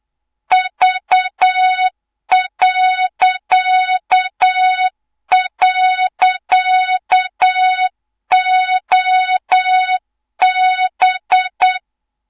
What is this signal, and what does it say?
Morse code, heard as a keyed tone, about 8 wpm, 3700 Hz, with noise. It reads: V..OB
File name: Morze-HA2MN / hang / 24 GYAKORLAT